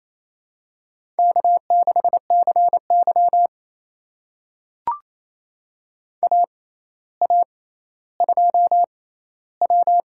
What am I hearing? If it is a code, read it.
Morse code, heard as a keyed tone, about 28 wpm, spaced as for 14 wpm, 700 Hz. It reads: K6CY AA2W
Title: USA Call Signs - 4 Characters - 6x Character Spacing - 28wpm